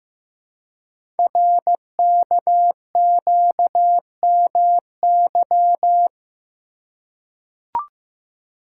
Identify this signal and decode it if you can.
Morse code, heard as a keyed tone, about 15 wpm, 700 Hz. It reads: RKQMY
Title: Code Groups - 5 Letter:Numbers - 15wpm